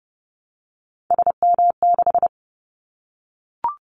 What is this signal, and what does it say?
Morse code, heard as a keyed tone, about 30 wpm, 700 Hz. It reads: SM6
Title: Callsign Prefixes for Top 25 DXCC - 30wpm